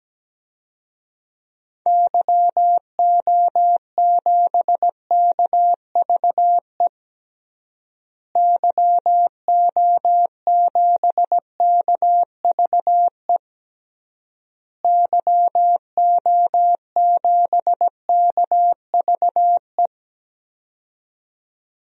Morse code, heard as a keyed tone, about 17 wpm, 700 Hz. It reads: YO7KVE YO7KVE YO7KVE
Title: DXCC-100 Call Signs - 6 Characters - Repeated 3x - 17wpm